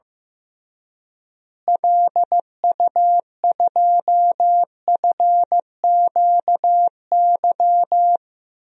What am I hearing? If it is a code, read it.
Morse code, heard as a keyed tone, about 15 wpm, 700 Hz. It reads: LU2FQY
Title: DXCC-100 Call Signs - 6 Characters - Repeated 2x - 15wpm